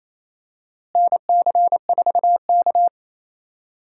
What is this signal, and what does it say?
Morse code, heard as a keyed tone, about 28 wpm, 700 Hz. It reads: NC4K
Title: USA Call Signs - 4 Characters - Repeated 2x - 28wpm